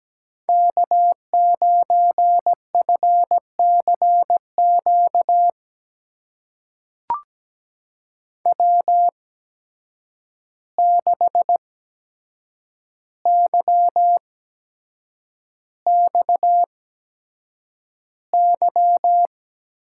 Morse code, heard as a keyed tone, about 17 wpm, spaced as for 7 wpm, 700 Hz. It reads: K9FCQ W6YXY
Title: USA Call Signs - 5 Characters - 8x Character Spacing - 17wpm